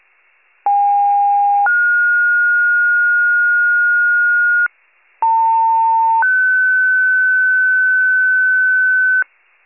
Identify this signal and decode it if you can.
Two-tone page, heard as a one-second tone, then a three-second tone, about 810 then 1430 Hz, about 900 then 1500 Hz. It is heard from 0.7 to 9.2 s.